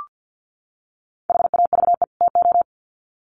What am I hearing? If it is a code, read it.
Morse code, heard as a keyed tone, about 50 wpm, 700 Hz. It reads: HAVE TO
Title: Sets of 2 Words 50wpm